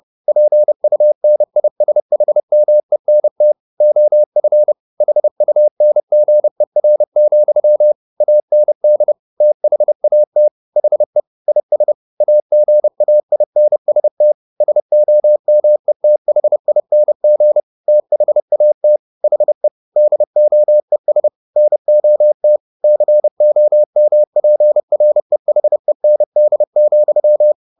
Morse code, heard as a keyed tone, about 30 wpm, 600 Hz. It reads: PUNISHMENT OF HUNGER, AND THAT HE IS AGAINST SOMETHING THAT HE DOES NOT COMPREHEND,